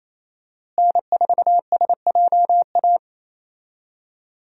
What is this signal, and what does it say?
Morse code, heard as a keyed tone, about 28 wpm, 700 Hz. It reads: N4SJA
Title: USA Call Signs - 5 Characters - Repeated 2x - 28wpm